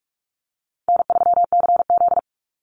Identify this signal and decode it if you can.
Morse code, heard as a keyed tone, about 44 wpm, 700 Hz. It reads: N3CZ
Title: Easy Call Signs - 45wpm